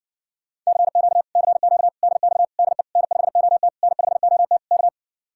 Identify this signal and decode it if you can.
Morse code, heard as a keyed tone, about 59 wpm, 700 Hz. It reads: CQ CQ DX DE N5OT N5OT K